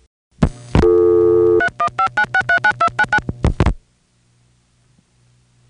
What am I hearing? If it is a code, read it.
DTMF, heard as telephone touch tones, tones of about 80 ms. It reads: 3159639299